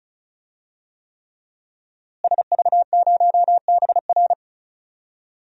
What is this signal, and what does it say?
Morse code, heard as a keyed tone, about 36 wpm, 700 Hz. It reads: SV0BR
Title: DXCC-100 Call Signs - 5 Characters - Repeated 3x - 35wpm